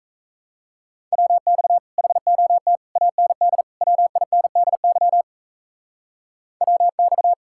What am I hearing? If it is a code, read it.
Morse code, heard as a keyed tone, about 34 wpm, 700 Hz. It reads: WX HOT AND WINDY WX
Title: Speed-Racing Common QSO Phrases Encore - 28wpm